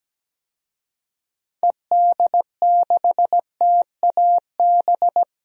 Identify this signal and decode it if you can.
Morse code, heard as a keyed tone, about 17 wpm, 700 Hz. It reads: ED6TAB
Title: DXCC-100 Call Signs - 6 Characters - Repeated 3x - 17wpm